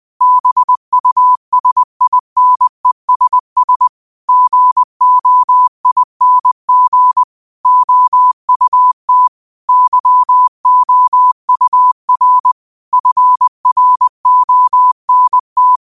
Morse code, heard as a keyed tone, about 20 wpm, 1000 Hz. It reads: BUSINESS GOING OUT YOUR FRONT
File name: qotdfiles / 20wpm